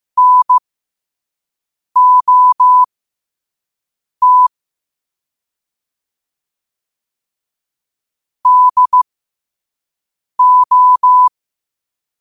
Morse code, heard as a keyed tone, about 15 wpm, spaced as for 5 wpm, 1000 Hz. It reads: NOT DO
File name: qotdfiles / 5wpm